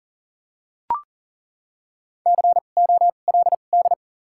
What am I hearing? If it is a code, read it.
Morse code, heard as a keyed tone, about 40 wpm, 700 Hz. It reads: C O L D
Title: 2x Character Spacing - Top 500 Words - 40wpm